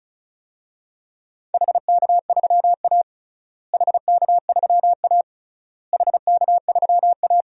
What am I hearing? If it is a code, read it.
Morse code, heard as a keyed tone, about 36 wpm, 700 Hz. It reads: HK3A HK3A HK3A
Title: DXCC-100 Call Signs - 4 Characters - Repeated 3x - 35wpm